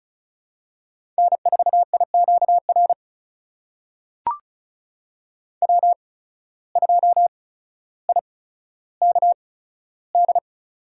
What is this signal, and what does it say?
Morse code, heard as a keyed tone, about 36 wpm, spaced as for 15 wpm, 700 Hz. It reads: N4IQR W2IKD
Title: USA Call Signs - 5 Characters - 8x Character Spacing - 35wpm